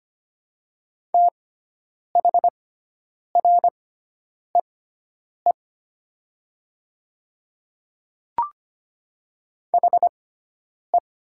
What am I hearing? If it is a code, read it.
Morse code, heard as a keyed tone, about 25 wpm, spaced as for 9 wpm, 700 Hz. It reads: THREE HE
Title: 6x Character Spacing - Top 200 Words - 25wpm